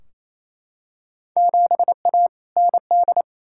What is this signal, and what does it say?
Morse code, heard as a keyed tone, about 28 wpm, 700 Hz. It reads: 7A ND
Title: Field Day Exchanges Encore - 28wpm